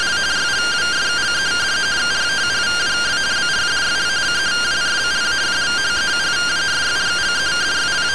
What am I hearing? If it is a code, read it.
Two-tone teleprinter signal, about 170 Hz shift.